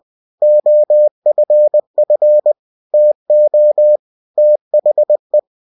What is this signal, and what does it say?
Morse code, heard as a keyed tone, about 20 wpm, 600 Hz. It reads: OFF TO THE
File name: cw-books / winnie-the-pooh / winnie-20wpm-page-0110